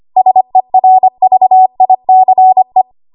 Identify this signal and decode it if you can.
Morse code, heard as a keyed tone, about 25 wpm, 750 Hz. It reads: SERVICE
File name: Morse / 25WPM